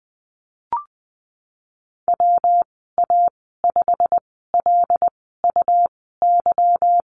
Morse code, heard as a keyed tone, about 20 wpm, spaced as for 15 wpm, 700 Hz. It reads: WA5LUY
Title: 179 Lesson 29g - Slash - Call Signs 20wpm